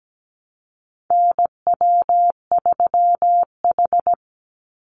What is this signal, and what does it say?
Morse code, heard as a keyed tone, about 17 wpm, 700 Hz. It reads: NW3H
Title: USA Call Signs 17wpm